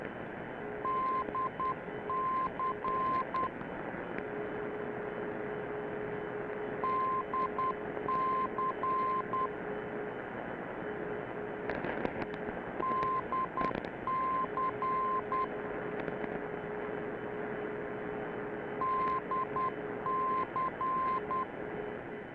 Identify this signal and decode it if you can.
Morse code, heard as a keyed tone, about 10 wpm, 1040 Hz, with radio noise. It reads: DC DC DC DC